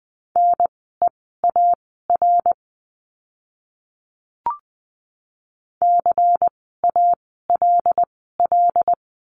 Morse code, heard as a keyed tone, about 20 wpm, spaced as for 14 wpm, 700 Hz. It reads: NEAR CALL
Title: 094 Lesson 16f - Letter c - Words 20wpm